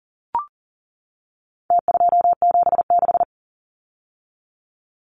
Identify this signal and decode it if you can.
Morse code, heard as a keyed tone, about 40 wpm, 700 Hz. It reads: T276
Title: Numbers - Contest Style - 4-Digit 40wpm